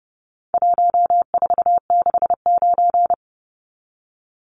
Morse code, heard as a keyed tone, about 30 wpm, 700 Hz. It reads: 1469
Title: Numbers - 4-Digit 30wpm